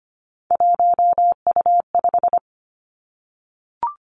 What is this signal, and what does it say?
Morse code, heard as a keyed tone, about 25 wpm, 700 Hz. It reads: 1U5
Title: Three Challenging Letter-Number Combinations 25wpm